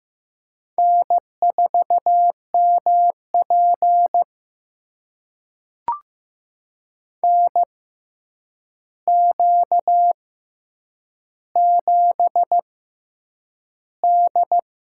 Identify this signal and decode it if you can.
Morse code, heard as a keyed tone, about 15 wpm, spaced as for 8 wpm, 700 Hz. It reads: N4MP NQ7D
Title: USA Call Signs - 4 Characters - 6x Character Spacing - 15wpm